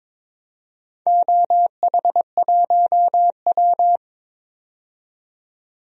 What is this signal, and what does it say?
Morse code, heard as a keyed tone, about 21 wpm, 700 Hz. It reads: OH1W
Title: DXCC-100 Call Signs - 4 Characters - 22wpm